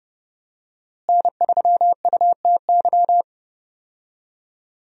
Morse code, heard as a keyed tone, about 30 wpm, 700 Hz. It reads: N3UTY
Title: USA Call Signs - 5 Characters - Repeated 2x - 30wpm